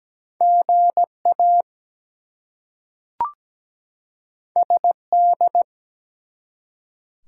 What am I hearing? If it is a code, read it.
Morse code, heard as a keyed tone, about 17 wpm, 700 Hz. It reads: GA SD
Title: US States and Canadian Provinces 17wpm